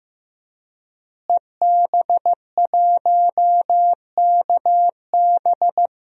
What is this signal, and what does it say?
Morse code, heard as a keyed tone, about 15 wpm, 700 Hz. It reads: EB1KB
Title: DXCC-100 Call Signs - 5 Characters - Repeated 3x - 15wpm